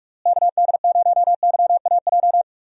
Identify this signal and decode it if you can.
Morse code, heard as a keyed tone, about 41 wpm, 700 Hz. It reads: KD0YAJ
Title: USA Call Signs - 6 Characters - Repeated 3x - 45wpm